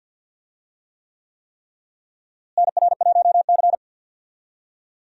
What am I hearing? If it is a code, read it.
Morse code, heard as a keyed tone, about 52 wpm, 700 Hz. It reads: NR1C
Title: USA Call Signs - 4 Characters - Repeated 3x - 50wpm